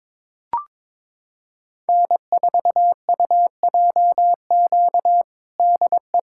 Morse code, heard as a keyed tone, about 22 wpm, 700 Hz. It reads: N4UJQ DE
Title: USA Callsign DE Callsign - 5 Characters - 22wpm